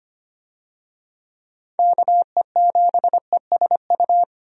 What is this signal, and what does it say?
Morse code, heard as a keyed tone, about 25 wpm, 700 Hz. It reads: KE7ESU
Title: USA Call Signs - 6 Characters - 25wpm